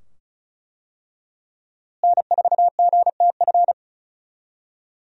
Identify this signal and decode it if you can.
Morse code, heard as a keyed tone, about 34 wpm, 700 Hz. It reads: N4GTF
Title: USA Call Signs - 5 Characters - Repeated 2x - 35wpm